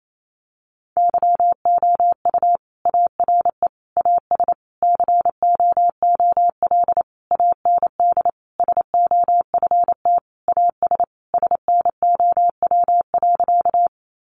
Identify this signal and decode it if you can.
Morse code, heard as a keyed tone, about 28 wpm, 700 Hz. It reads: YOU ARE AS COOL AND SOFT AS SNOW.
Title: Sentences from Top 1500 Words 28wpm